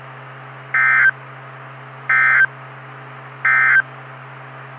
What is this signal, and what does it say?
Emergency Alert System (SAME) tones, heard as harsh buzzing bursts. It is heard from 0.6 to 3.9 s.
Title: Saline, AK SKYWARN Repeater System - W5RHS